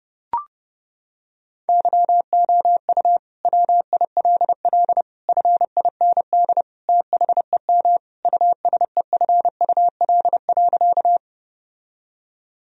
Morse code, heard as a keyed tone, about 30 wpm, 700 Hz. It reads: YOU WILL FIND THEM USEFUL.
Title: Sentences from Top 1000 Words - Review Part 4 - 30wpm